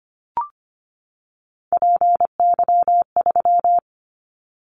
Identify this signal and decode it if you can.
Morse code, heard as a keyed tone, about 25 wpm, 700 Hz. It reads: PY3
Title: Callsign Prefixes for Top 50 DXCC - 25wpm